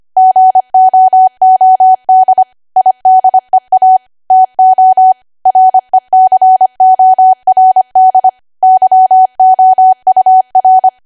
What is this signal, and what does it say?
Morse code, heard as a keyed tone, about 25 wpm, 750 Hz. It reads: GOOD IDEA TO RECORD YOUR
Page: 25 WPM Code Archive